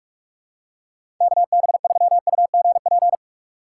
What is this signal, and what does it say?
Morse code, heard as a keyed tone, about 41 wpm, 700 Hz. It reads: KB3UGP
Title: USA Call Signs - 6 Characters - 4x Character Spacing - 45wpm